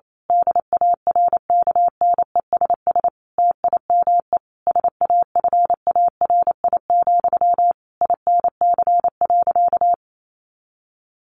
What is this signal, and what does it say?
Morse code, heard as a keyed tone, about 28 wpm, 700 Hz. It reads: DARKNESS TIME SAFARI, INC.
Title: A Sound Of Thunder 28wpm